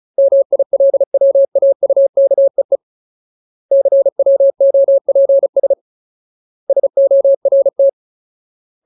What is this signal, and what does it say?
Morse code, heard as a keyed tone, about 34 wpm, 550 Hz. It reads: MILWAUKEE CWOPS SORT
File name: Ultimate Word List / 35WPM Ultimate